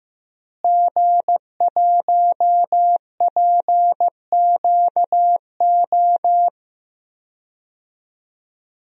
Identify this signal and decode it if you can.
Morse code, heard as a keyed tone, about 15 wpm, 700 Hz. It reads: G1PQO
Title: DXCC-100 Call Signs - 5 Characters - Repeated 3x - 15wpm